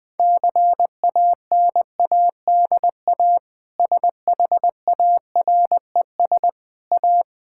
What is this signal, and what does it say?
Morse code, heard as a keyed tone, about 20 wpm, 700 Hz. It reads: CANADA SHARES A
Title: Sentences with 4-Word Phrases - Spoken Beginning - 20wpm